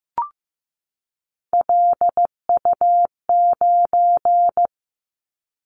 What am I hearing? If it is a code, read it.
Morse code, heard as a keyed tone, about 15 wpm, 700 Hz. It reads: LU9
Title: Callsign Prefixes for Top 100 DXCC - 15wpm